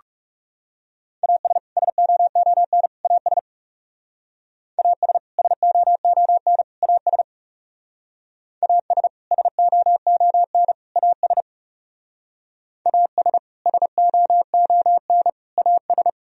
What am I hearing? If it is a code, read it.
Morse code, heard as a keyed tone, about 38 wpm, 700 Hz. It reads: AS SOON AS AS SOON AS AS SOON AS AS SOON AS